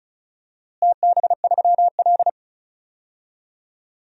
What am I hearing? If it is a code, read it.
Morse code, heard as a keyed tone, about 34 wpm, 700 Hz. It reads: TB3L
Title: DXCC-100 Call Signs - 4 Characters - Repeated 3x - 35wpm